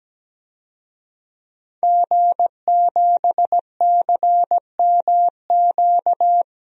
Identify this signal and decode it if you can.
Morse code, heard as a keyed tone, about 17 wpm, 700 Hz. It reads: G7CMQ
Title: DXCC-100 Call Signs - 5 Characters - Repeated 2x - 17wpm